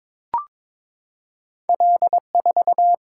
Morse code, heard as a keyed tone, about 22 wpm, 700 Hz. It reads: L4
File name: practice / 22wpm / Two Letter-Number Combinations - 22wpm